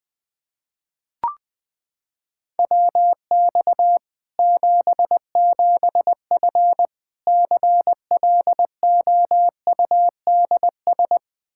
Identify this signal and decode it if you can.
Morse code, heard as a keyed tone, about 21 wpm, 700 Hz. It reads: WX 77F CLOUDS
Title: Practice QSO - Exchange 2 - Info - 20wpm